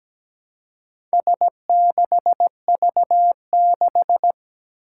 Morse code, heard as a keyed tone, about 17 wpm, 700 Hz. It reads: S6V6
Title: Four Challenging Letter-Number Combinations 17wpm